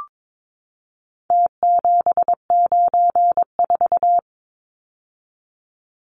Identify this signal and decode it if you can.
Morse code, heard as a keyed tone, about 22 wpm, 700 Hz. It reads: T794
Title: Numbers - Contest Style - 4-Digit 22wpm